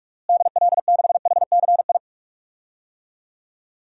Morse code, heard as a keyed tone, about 46 wpm, 700 Hz. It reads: DL6HCI
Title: DXCC-100 Call Signs - 6 Characters - Repeated 3x - 45wpm